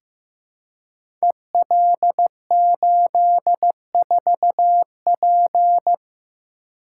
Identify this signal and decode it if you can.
Morse code, heard as a keyed tone, about 15 wpm, 700 Hz. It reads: EL84P
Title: Code Groups - 5 Letter:Numbers - 15wpm